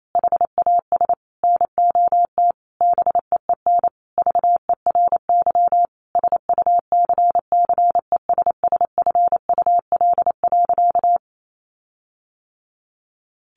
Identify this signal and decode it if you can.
Morse code, heard as a keyed tone, about 28 wpm, 700 Hz. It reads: HAS NOT BEEN VERY SUCCESSFUL.
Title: Sentences from Top 1300 Words 28wpm